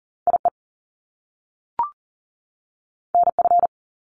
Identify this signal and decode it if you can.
Morse code, heard as a keyed tone, about 40 wpm, 700 Hz. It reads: IE NF
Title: Top Two Letter Combinations in Words 40wpm